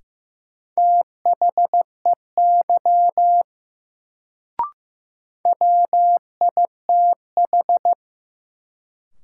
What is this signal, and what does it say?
Morse code, heard as a keyed tone, about 15 wpm, 700 Hz. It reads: THEY WITH
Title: Sentences from Top 100 Words - New Word Drill - 15wpm